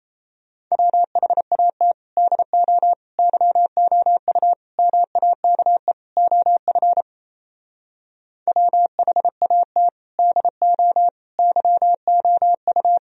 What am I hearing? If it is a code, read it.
Morse code, heard as a keyed tone, about 38 wpm, 700 Hz. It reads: WHAT DO YOU MAKE OF WHAT DO YOU